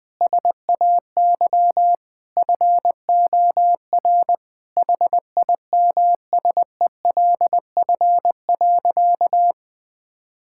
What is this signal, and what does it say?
Morse code, heard as a keyed tone, about 19 wpm, 700 Hz. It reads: SAY FOR HIMSELF.
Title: Sentences from Top 1600 Words 20wpm